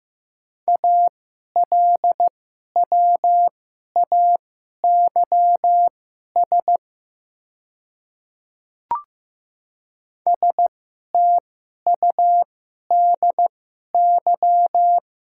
Morse code, heard as a keyed tone, about 15 wpm, spaced as for 11 wpm, 700 Hz. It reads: ALWAYS STUDY
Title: 2x Character Spacing - Top 300 Words - 15wpm